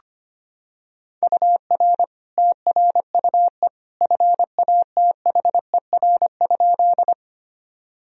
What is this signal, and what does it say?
Morse code, heard as a keyed tone, about 25 wpm, 700 Hz. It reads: UR TRUE FATHER?
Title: Sentences from Top 400 Words and Abbreviations 25wpm